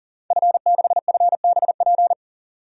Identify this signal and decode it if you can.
Morse code, heard as a keyed tone, about 37 wpm, 700 Hz. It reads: F6FBP
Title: DXCC-100 Call Signs - 5 Characters - 40wpm